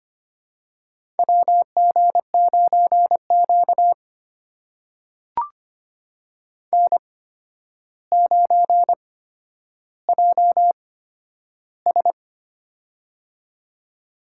Morse code, heard as a keyed tone, about 25 wpm, spaced as for 11 wpm, 700 Hz. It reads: WG9Q N9JS
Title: USA Call Signs - 4 Characters - 8x Character Spacing - 25wpm